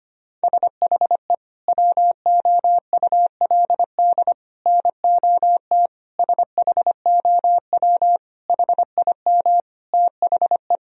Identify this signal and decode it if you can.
Morse code, heard as a keyed tone, about 25 wpm, 700 Hz. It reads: SHE WOULD NOT SHOW HIM THE
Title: Sentences from Top 1000 Words - Review Part 3 - 25wpm